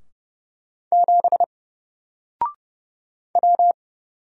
Morse code, heard as a keyed tone, about 30 wpm, 700 Hz. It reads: 7 W
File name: Single-Letter-Number-30wpm